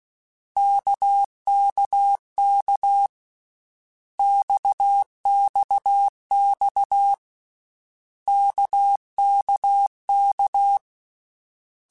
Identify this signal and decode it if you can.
Morse code, heard as a keyed tone, about 16 wpm, 800 Hz. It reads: KKK XXX KKK
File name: Basic Letter Groups By7Groups / RKPX / RKPX@13